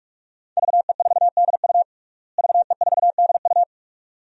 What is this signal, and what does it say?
Morse code, heard as a keyed tone, about 46 wpm, 700 Hz. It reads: VE4DU VE4DU
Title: DXCC-100 Call Signs - 5 Characters - Repeated 2x - 45wpm